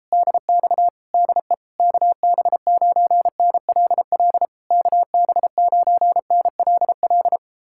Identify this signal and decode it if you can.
Morse code, heard as a keyed tone, about 37 wpm, 700 Hz. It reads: DX DE KB9NLL KB9NLL